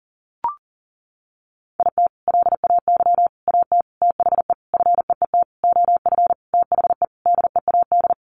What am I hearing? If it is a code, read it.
Morse code, heard as a keyed tone, about 40 wpm, 700 Hz. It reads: IT LAY AT THE FEET OF THE DEAD
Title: Sentences from Top 1500 Words 40wpm